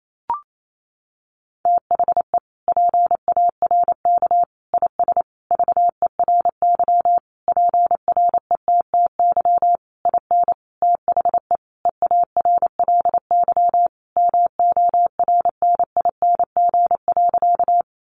Morse code, heard as a keyed tone, about 28 wpm, 700 Hz. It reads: THE PARK IS VERY PRETTY IN THE EARLY MORNING.
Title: Sentences from Top 1300 Words 28wpm